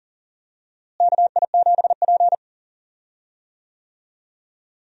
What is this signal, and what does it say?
Morse code, heard as a keyed tone, about 40 wpm, 700 Hz. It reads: KI7P
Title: USA Call Signs - 4 Characters - Repeated 2x - 40wpm